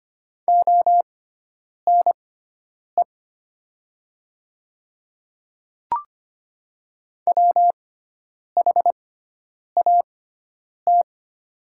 Morse code, heard as a keyed tone, about 25 wpm, spaced as for 9 wpm, 700 Hz. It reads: ONE WHAT
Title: 6x Character Spacing - Top 100 Words - 25wpm